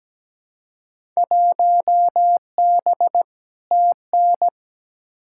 Morse code, heard as a keyed tone, about 17 wpm, 700 Hz. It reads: 1B TN
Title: Field Day Exchanges - 17wpm